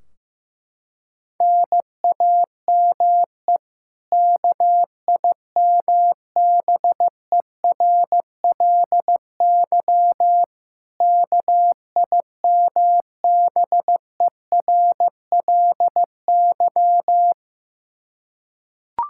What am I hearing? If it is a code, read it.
Morse code, heard as a keyed tone, about 15 wpm, 700 Hz. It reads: NAME KIMBERLY KIMBERLY
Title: Practice QSO - Exchange 1 - Beginning - 15wpm